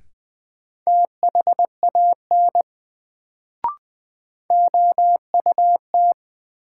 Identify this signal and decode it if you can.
Morse code, heard as a keyed tone, about 21 wpm, 700 Hz. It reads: THAN OUT
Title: Sets of 2 Words - New Word Drill - 20wpm